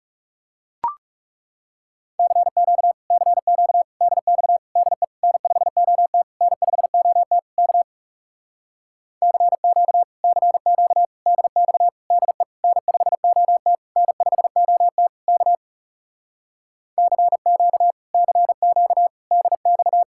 Morse code, heard as a keyed tone, about 37 wpm, 700 Hz. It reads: CQ CQ DX DE N5OT N5OT K CQ CQ DX DE N5OT N5OT K CQ CQ DX
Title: Speed-Racing Common QSO Phrases - 30wpm